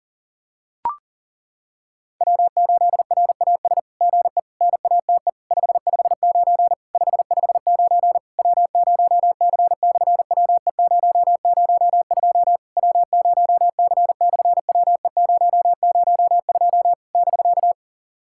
Morse code, heard as a keyed tone, about 40 wpm, 700 Hz. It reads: W8RAS GE NATE 559 559 W0C/WE002 W0C/WE002 <BK>
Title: SOTA QSO Practice 40wpm